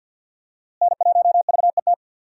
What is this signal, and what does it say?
Morse code, heard as a keyed tone, about 49 wpm, 700 Hz. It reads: N1VET
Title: USA Call Signs - 5 Characters - Repeated 3x - 50wpm